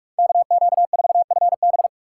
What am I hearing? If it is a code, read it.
Morse code, heard as a keyed tone, about 42 wpm, 700 Hz. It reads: KQ4FB
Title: USA Call Signs - 5 Characters - 3x Character Spacing - 45wpm